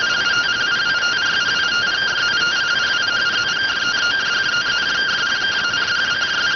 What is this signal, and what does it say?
Two-tone teleprinter signal, about 170 Hz shift.